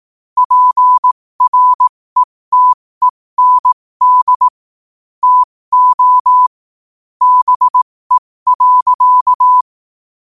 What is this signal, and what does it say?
Morse code, heard as a keyed tone, about 18 wpm, spaced as for 15 wpm, 1000 Hz. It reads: PRETEND TO BE.
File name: qotdfiles / 15wpm